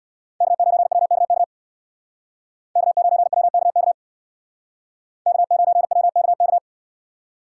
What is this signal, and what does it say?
Morse code, heard as a keyed tone, about 64 wpm, 700 Hz. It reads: K9WKK K9WKK K9WKK